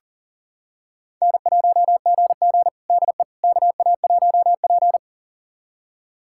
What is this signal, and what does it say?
Morse code, heard as a keyed tone, about 37 wpm, 690 Hz. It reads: N1GG DE KA1P
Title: USA Callsign DE Callsign - 4 Characters - 40wpm